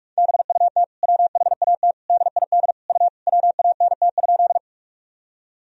Morse code, heard as a keyed tone, about 42 wpm, 700 Hz. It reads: BUT WHAT DID U WANT?
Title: Sentences from Top 200 Words and Abbreviations 45wpm